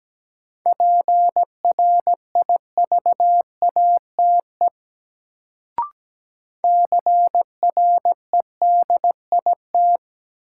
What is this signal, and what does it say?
Morse code, heard as a keyed tone, about 17 wpm, 700 Hz. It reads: PRIVATE CREDIT
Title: Sentences from Top 1000 Words - New Word Drill - 17wpm